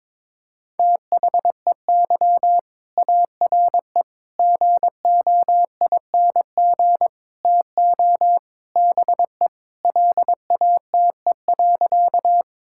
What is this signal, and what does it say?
Morse code, heard as a keyed tone, about 22 wpm, 700 Hz. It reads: THEY ARE GOING TO BE LATE.
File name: practice / 22wpm / Sentences with 4-Word Phrases - 22wpm